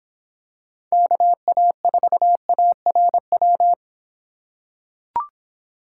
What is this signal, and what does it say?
Morse code, heard as a keyed tone, about 26 wpm, 700 Hz. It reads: KA4ARW